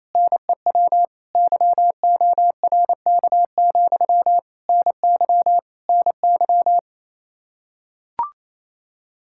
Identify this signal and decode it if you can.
Morse code, heard as a keyed tone, about 28 wpm, 700 Hz. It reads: NEW YORK, NY NY
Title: Practice QSO - Exchange 1 - Beginning - 28wpm